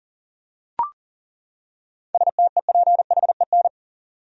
Morse code, heard as a keyed tone, about 37 wpm, 700 Hz. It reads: STEPHEN
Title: USA Common Names - 40wpm